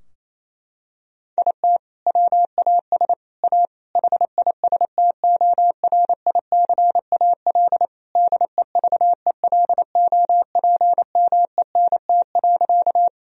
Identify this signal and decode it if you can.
Morse code, heard as a keyed tone, about 28 wpm, 700 Hz. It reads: IT WAS A HISTORICAL DEVELOPMENT.
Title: Sentences from Top 1000 Words - Review Part 4 - 28wpm